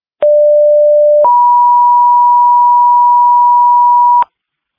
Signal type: two-tone page